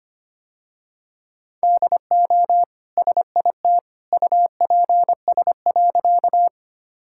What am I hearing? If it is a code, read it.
Morse code, heard as a keyed tone, about 25 wpm, 700 Hz. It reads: DO SIT UPS.
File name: Easy Sentences with Hints - 25wpm